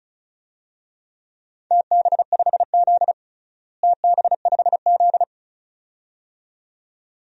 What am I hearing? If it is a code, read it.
Morse code, heard as a keyed tone, about 36 wpm, 700 Hz. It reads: TB5Z TB5Z
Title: DXCC-100 Call Signs - 4 Characters - Repeated 2x - 35wpm